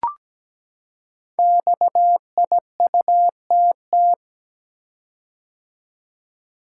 Morse code, heard as a keyed tone, about 17 wpm, 700 Hz. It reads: XIUTT
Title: Code Groups - 5 Letters - 17wpm